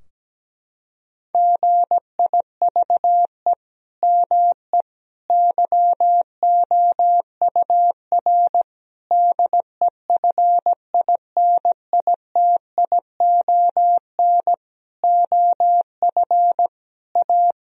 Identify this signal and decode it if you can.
Morse code, heard as a keyed tone, about 17 wpm, 700 Hz. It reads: GIVE ME YOUR DEFINITION OF A
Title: Sentences from Top 1000 Words - Review Part 4 - 17wpm